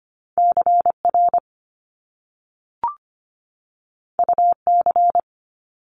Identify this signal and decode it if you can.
Morse code, heard as a keyed tone, about 25 wpm, 700 Hz. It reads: CR UC
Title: Top Two Letter Combinations in Words 25wpm